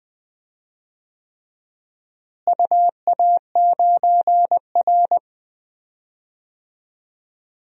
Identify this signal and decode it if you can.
Morse code, heard as a keyed tone, about 21 wpm, 700 Hz. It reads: UA9R
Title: DXCC-100 Call Signs - 4 Characters - Repeated 3x - 20wpm